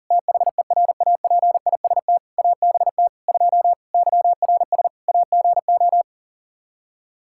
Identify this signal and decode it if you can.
Morse code, heard as a keyed tone, about 40 wpm, 700 Hz. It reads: THERAPIST ABT 2 YRS AGO
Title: K1USN SST QSO Practice 40wpm